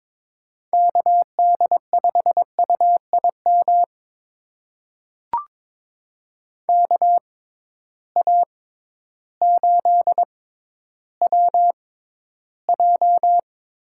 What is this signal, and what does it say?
Morse code, heard as a keyed tone, about 22 wpm, spaced as for 12 wpm, 700 Hz. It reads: KD5UIM KA8WJ
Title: USA Call Signs - 6 Characters - 6x Character Spacing - 22wpm